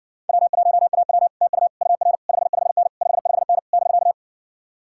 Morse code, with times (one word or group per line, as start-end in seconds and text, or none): 0.3-1.3 W1AW
1.4-1.7 TU
1.8-2.2 UR
2.3-2.9 55N
3.0-3.6 55N
3.7-4.1 <BK>